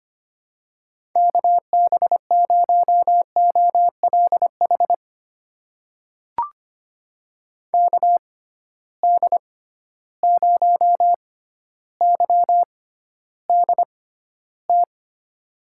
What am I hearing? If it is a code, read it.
Morse code, heard as a keyed tone, about 25 wpm, spaced as for 13 wpm, 700 Hz. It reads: KB0OLH KD0YDT
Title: USA Call Signs - 6 Characters - 6x Character Spacing - 25wpm